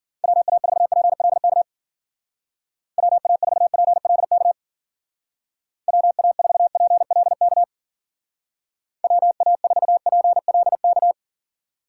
Morse code, heard as a keyed tone, about 48 wpm, 700 Hz. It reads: WA4PLK WA4PLK WA4PLK WA4PLK
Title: Speed Racing - USA 2x3 Call Signs 40wpm